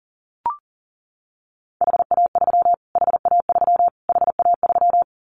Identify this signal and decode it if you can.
Morse code, heard as a keyed tone, about 37 wpm, 700 Hz. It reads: HA3 HA3 HA3
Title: Callsign Prefixes for Top 25 DXCC - Repeated 3x - 40wpm